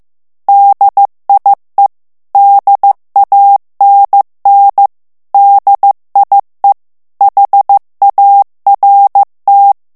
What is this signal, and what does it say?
Morse code, heard as a keyed tone, about 15 wpm, 800 Hz. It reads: DIE DANN DIE HART
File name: morse mp3 / ALLEMAND texte clair 1 vitesse 14 mots minute